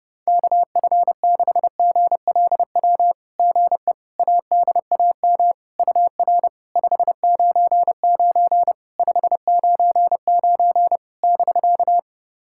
Morse code, heard as a keyed tone, about 30 wpm, 700 Hz. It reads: KF6GLW GE ADAM UR 599 599 <BK>
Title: SOTA QSO Practice 30wpm